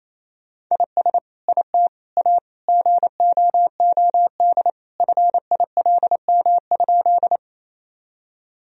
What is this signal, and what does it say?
Morse code, heard as a keyed tone, about 28 wpm, 700 Hz. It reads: IS IT A GOOD FILM?
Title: Sentences from Top 1000 Words - Review Part 3 - 28wpm